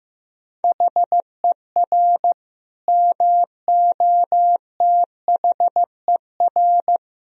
Morse code, heard as a keyed tone, about 15 wpm, 700 Hz. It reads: HER MOTHER
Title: Sentences from Top 1000 Words - Review Part 4 - 15wpm